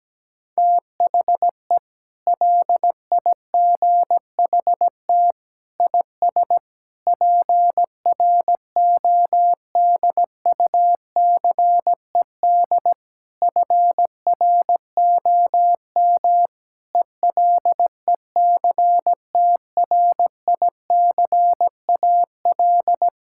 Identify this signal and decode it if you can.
Morse code, heard as a keyed tone, about 17 wpm, 700 Hz. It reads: THE LIGHT IS PRODUCED FROM ELECTRICAL
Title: Sentences from Top 1000 Words - Review Part 6 - 17wpm